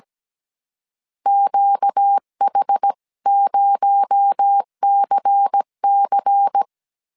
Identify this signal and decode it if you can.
Morse code, heard as a keyed tone, about 17 wpm, 800 Hz. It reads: QH0CC
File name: 16wpm